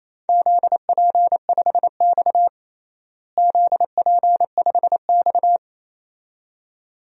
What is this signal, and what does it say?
Morse code, heard as a keyed tone, about 28 wpm, 700 Hz. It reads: ZP5X ZP5X
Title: DXCC-100 Call Signs - 4 Characters - Repeated 2x - 28wpm